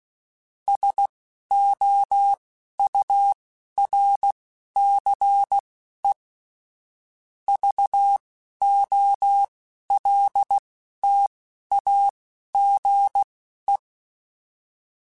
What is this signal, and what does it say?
Morse code, heard as a keyed tone, about 16 wpm, spaced as for 11 wpm, 800 Hz. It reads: SOURCE VOLTAGE
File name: Technical Subjects / Technical Subjects-mp3s / XC @10